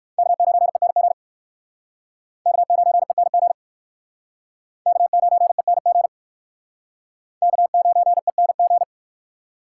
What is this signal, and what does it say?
Morse code, heard as a keyed tone, about 59 wpm, 700 Hz. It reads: K9ENG K9ENG K9ENG K9ENG